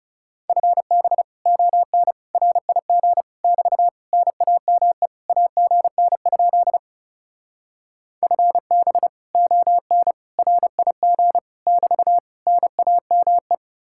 Morse code, heard as a keyed tone, about 37 wpm, 700 Hz. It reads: FB ON RIG <BT> NAME AGN? FB ON RIG <BT> NAME
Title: Speed-Racing Common QSO Phrases Encore - 30wpm